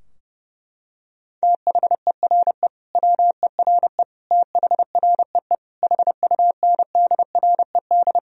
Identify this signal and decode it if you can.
Morse code, heard as a keyed tone, about 30 wpm, 700 Hz. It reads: THERE WERE THREE HUNDRED
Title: Sentences from Top 1000 Words - Review Part 2 - 30wpm